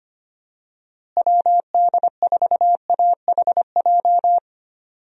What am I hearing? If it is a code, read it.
Morse code, heard as a keyed tone, about 25 wpm, 700 Hz. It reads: WD4AHJ
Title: USA Call Signs - 6 Characters - Repeated 2x - 25wpm